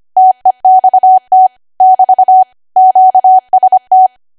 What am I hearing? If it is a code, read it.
Morse code, heard as a keyed tone, about 25 wpm, 750 Hz. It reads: TEXT <BT> QST